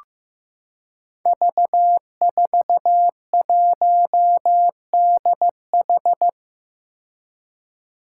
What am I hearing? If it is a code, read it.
Morse code, heard as a keyed tone, about 15 wpm, 700 Hz. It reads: V41DH
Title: Five Challenging Letter-Number Combinations 15wpm